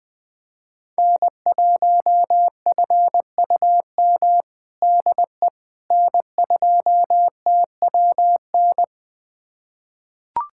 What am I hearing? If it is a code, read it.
Morse code, heard as a keyed tone, about 21 wpm, 700 Hz. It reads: N1FUM DE N2TWN
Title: USA Callsign DE Callsign - 5 Characters - 20wpm